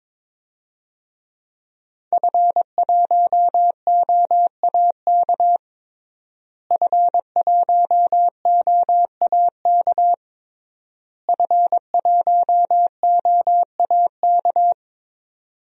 Morse code, heard as a keyed tone, about 22 wpm, 700 Hz. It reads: F1OAK F1OAK F1OAK
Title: DXCC-100 Call Signs - 5 Characters - Repeated 3x - 22wpm